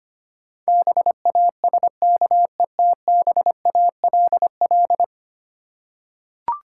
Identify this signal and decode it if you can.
Morse code, heard as a keyed tone, about 25 wpm, 700 Hz. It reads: BASKETBALL
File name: practice / 25wpm / Difficult Words with Hints - 25wpm